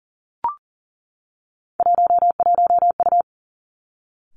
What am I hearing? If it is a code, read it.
Morse code, heard as a keyed tone, about 40 wpm, 700 Hz. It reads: 11U
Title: Three Challenging Letter-Number Combinations 40wpm